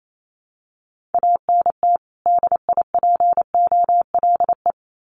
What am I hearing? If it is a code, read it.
Morse code, heard as a keyed tone, about 28 wpm, 700 Hz. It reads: ANT DIPOLE
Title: Common QSO Phrases 28wpm